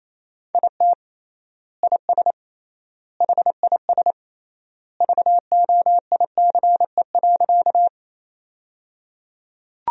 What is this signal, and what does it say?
Morse code, heard as a keyed tone, about 28 wpm, 700 Hz. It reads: IT IS HIS VOICE.
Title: 3x Word-Spacing - Sentences from Top 500 Words - 28wpm